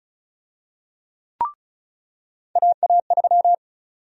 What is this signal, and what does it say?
Morse code, heard as a keyed tone, about 36 wpm, 700 Hz. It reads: AA3
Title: USA Call Sign Prefixes - 35wpm